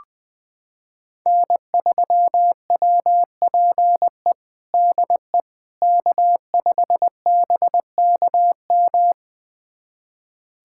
Morse code, heard as a keyed tone, about 20 wpm, 700 Hz. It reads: N3WPE DE K5BKM
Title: USA Callsign DE Callsign - 5 Characters - 20wpm